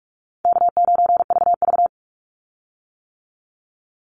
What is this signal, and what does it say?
Morse code, heard as a keyed tone, about 44 wpm, 700 Hz. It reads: K9VV
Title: Easy Call Signs - 45wpm